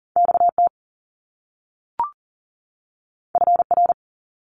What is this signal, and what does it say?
Morse code, heard as a keyed tone, about 41 wpm, 700 Hz. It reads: XT FR
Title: Top Two Letter Combinations in Words 40wpm